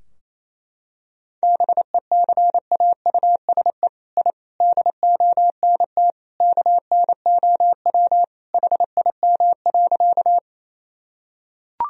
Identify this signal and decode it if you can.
Morse code, heard as a keyed tone, about 28 wpm, 700 Hz. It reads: BECAUSE I DONT KNOW HIM.
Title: Sentences from Top 1000 Words - Review Part 3 - 28wpm